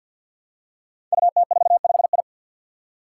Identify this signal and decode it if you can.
Morse code, heard as a keyed tone, about 49 wpm, 700 Hz. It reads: UT45I
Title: Five Challenging Letter-Number Combinations 50wpm